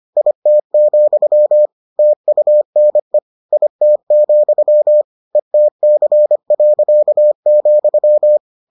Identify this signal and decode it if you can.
Morse code, heard as a keyed tone, about 25 wpm, 600 Hz. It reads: IT, TUNE IT, ETC.,